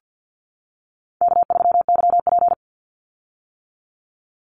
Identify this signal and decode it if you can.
Morse code, heard as a keyed tone, about 50 wpm, 700 Hz. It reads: K3YP
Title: Call Signs - 50wpm